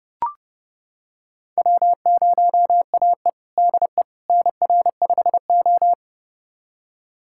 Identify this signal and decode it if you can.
Morse code, heard as a keyed tone, about 30 wpm, 700 Hz. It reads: W0AE DE NR5O
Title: USA Callsign DE Callsign - 4 Characters - 30wpm